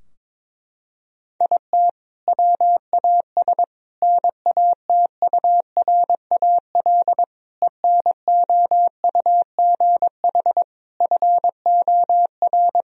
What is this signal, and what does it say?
Morse code, heard as a keyed tone, about 22 wpm, 700 Hz. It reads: IT WAS NATURAL ENOUGH FOR
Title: Sentences from Top 1000 Words - Review Part 4 - 22wpm